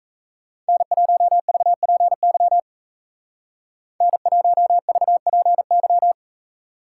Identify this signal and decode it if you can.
Morse code, heard as a keyed tone, about 34 wpm, 700 Hz. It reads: N1VPY N1VPY
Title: Speed Racing - USA 1x3 Call Signs 28wpm